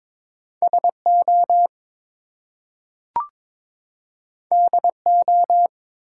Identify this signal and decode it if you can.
Morse code, heard as a keyed tone, about 22 wpm, 700 Hz. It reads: SO DO
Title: Two Letter Words - 22wpm